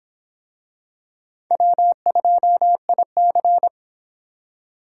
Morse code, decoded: W2IC

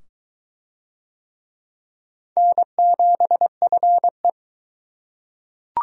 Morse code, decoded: N7FE

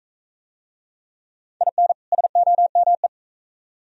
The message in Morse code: IN SOME

